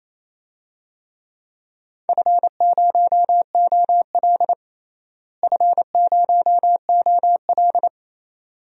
Morse code, heard as a keyed tone, about 28 wpm, 700 Hz. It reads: F0OL F0OL
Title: DXCC-100 Call Signs - 4 Characters - Repeated 2x - 28wpm